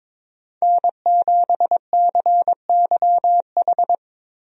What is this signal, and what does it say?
Morse code, heard as a keyed tone, about 22 wpm, 700 Hz. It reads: N7CYH
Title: USA Call Signs - 5 Characters - 22wpm